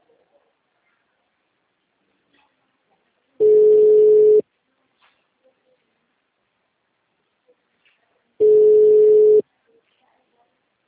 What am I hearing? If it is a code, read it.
Ringback tone (Europe, 425 Hz), with speech.